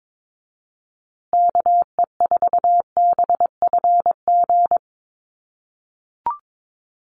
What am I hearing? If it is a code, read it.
Morse code, heard as a keyed tone, about 22 wpm, 700 Hz. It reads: KE4BFG
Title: USA Call Signs Repeated 3x 22wpm